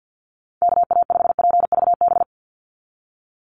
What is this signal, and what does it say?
Morse code, heard as a keyed tone, about 49 wpm, 700 Hz. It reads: KA5PVB
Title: NA SOTA Chaser Callsigns 50wpm